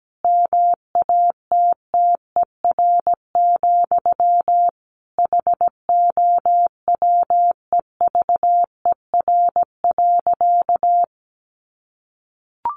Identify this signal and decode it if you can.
Morse code, heard as a keyed tone, about 17 wpm, 700 Hz. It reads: MATTER, HOWEVER.
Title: Sentences from Top 700 Words 17wpm